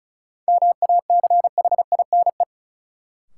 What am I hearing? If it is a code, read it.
Morse code, heard as a keyed tone, about 36 wpm, 700 Hz. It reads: MACHINE
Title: Sentences from Top 1000 Words - Review Part 2 - New Word Drill - 35wpm